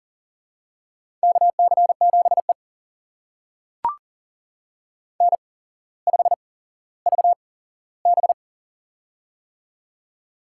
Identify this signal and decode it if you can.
Morse code, heard as a keyed tone, about 40 wpm, spaced as for 17 wpm, 700 Hz. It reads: KC7E N5VB
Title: USA Call Signs - 4 Characters - 8x Character Spacing - 40wpm